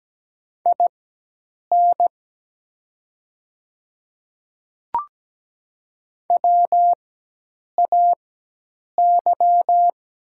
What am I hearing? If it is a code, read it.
Morse code, heard as a keyed tone, about 17 wpm, spaced as for 8 wpm, 700 Hz. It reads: IN WAY
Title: 4x Character Spacing - Top 100 Words - 17wpm